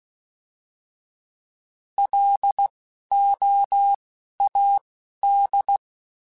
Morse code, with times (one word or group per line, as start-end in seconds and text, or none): 2.0-2.7 L
3.1-3.9 O
4.4-4.8 A
5.2-5.8 D